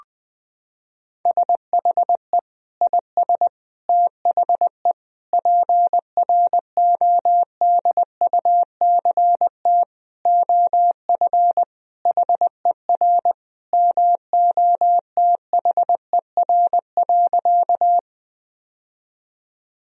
Morse code, heard as a keyed tone, about 20 wpm, 700 Hz. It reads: SHE IS THE PRODUCT OF HER MOTHER.
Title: Sentences from Top 1000 Words - Review Part 2 - 20wpm